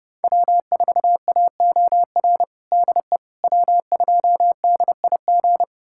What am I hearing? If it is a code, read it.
Morse code, heard as a keyed tone, about 30 wpm, 700 Hz. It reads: W4AOR DE W2DIG
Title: USA Callsign DE Callsign - 5 Characters - 30wpm